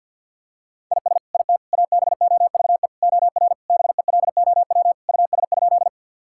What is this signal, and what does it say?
Morse code, heard as a keyed tone, about 50 wpm, 700 Hz. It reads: IS IT ABOVE OR BELOW US?
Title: Sentences from Top 1000 Words - Review Part 5 - 50wpm